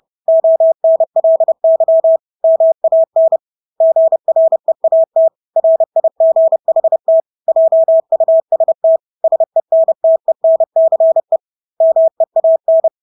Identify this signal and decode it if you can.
Morse code, heard as a keyed tone, about 30 wpm, 650 Hz. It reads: ONLY MAN GREAT RIGHT JUST SENTENCE MEAN
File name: Engelske ord 30-30 wpm